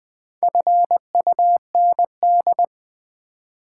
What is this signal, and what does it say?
Morse code, heard as a keyed tone, about 20 wpm, 700 Hz. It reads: FUND
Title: Build A Word - 4 Letters At A Time - 11-Letter Words - 20wpm